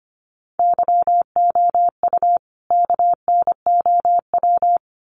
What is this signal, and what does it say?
Morse code, heard as a keyed tone, about 25 wpm, 700 Hz. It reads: YOU KNOW
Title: Sets of 2 Words 25wpm